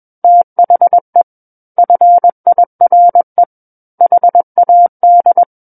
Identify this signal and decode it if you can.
Morse code, heard as a keyed tone, about 21 wpm, 700 Hz. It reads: THE FIRE HAD
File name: Book One / 12-Chapter 12 - 21 WPM